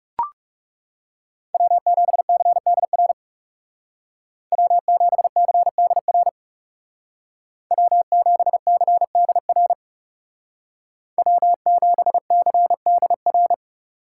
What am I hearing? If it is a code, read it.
Morse code, heard as a keyed tone, about 39 wpm, 700 Hz. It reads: W7CDR W7CDR W7CDR W7CDR